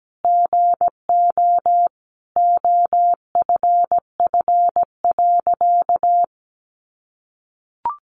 Morse code, heard as a keyed tone, about 17 wpm, 700 Hz. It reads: GO OFF.
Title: A Sound Of Thunder 17wpm